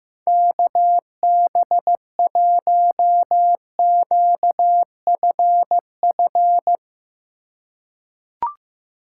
Morse code, heard as a keyed tone, about 15 wpm, 700 Hz. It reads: KB1QFF
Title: USA Call Signs - 6 Characters - 3x Character Spacing - 15wpm